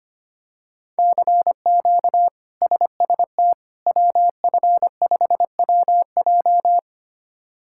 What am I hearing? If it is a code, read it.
Morse code, heard as a keyed tone, about 25 wpm, 700 Hz. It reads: CQ SST WF5WJ
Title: K1USN SST QSO Practice - 25wpm